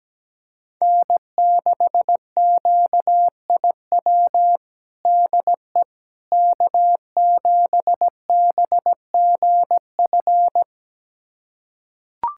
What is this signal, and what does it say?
Morse code, heard as a keyed tone, about 17 wpm, 700 Hz. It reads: N6QIW DE K7BGF
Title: USA Callsign DE Callsign - 5 Characters - 17wpm